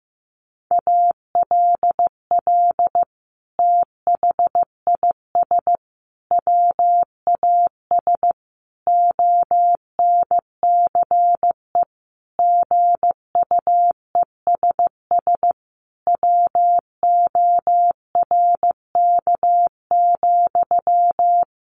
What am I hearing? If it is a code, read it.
Morse code, heard as a keyed tone, about 15 wpm, 700 Hz. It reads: ALL THIS WAS ONCE GUESS WORK,
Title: Sentences from Top 1500 Words 15wpm